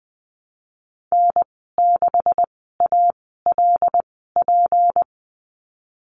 Morse code, heard as a keyed tone, about 20 wpm, 700 Hz. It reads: N 6 A L P
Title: 154 Lesson 25g - Letter p - Call Signs 20wpm